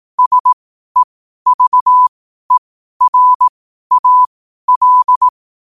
Morse code, heard as a keyed tone, about 18 wpm, spaced as for 12 wpm, 1000 Hz. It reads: SEVERAL
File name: qotdfiles / 12wpm